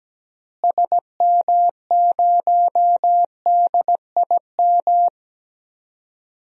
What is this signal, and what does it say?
Morse code, heard as a keyed tone, about 17 wpm, 700 Hz. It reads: SM0DIM
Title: DXCC-100 Call Signs - 6 Characters - Repeated 2x - 17wpm